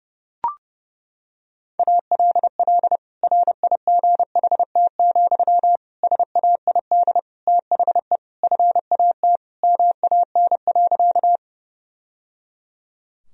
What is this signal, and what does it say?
Morse code, heard as a keyed tone, about 30 wpm, 700 Hz. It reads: ALL RIGHT, SAID THE FAT MAN.
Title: Sentences from Top 1000 Words - Review Part 4 - 30wpm